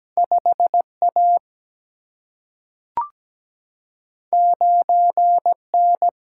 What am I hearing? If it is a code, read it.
Morse code, heard as a keyed tone, about 17 wpm, 700 Hz. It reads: 5A 9N
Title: Single-Letter Single-Number 17wpm